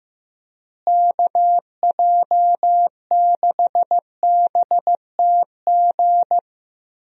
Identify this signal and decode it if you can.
Morse code, heard as a keyed tone, about 15 wpm, 700 Hz. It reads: KJ6BTG
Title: USA Call Signs - 6 Characters - Repeated 3x - 15wpm